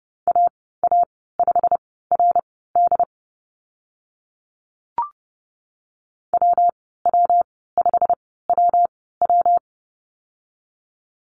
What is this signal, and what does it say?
Morse code, heard as a keyed tone, about 30 wpm, spaced as for 17 wpm, 700 Hz. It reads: AA5RD WW5WW
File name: Lesson 19g - Letter w - Call Signs 30wpm